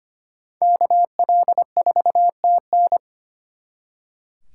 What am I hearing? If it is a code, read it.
Morse code, heard as a keyed tone, about 32 wpm, 700 Hz. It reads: KL4TN